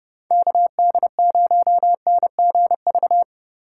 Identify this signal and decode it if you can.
Morse code, heard as a keyed tone, about 30 wpm, 700 Hz. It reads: KD0NGV